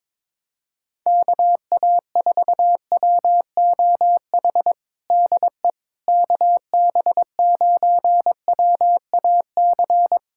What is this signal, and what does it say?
Morse code, heard as a keyed tone, about 21 wpm, 700 Hz. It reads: KA4WOH DE KB9WAC
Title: USA Callsign DE Callsign 22wpm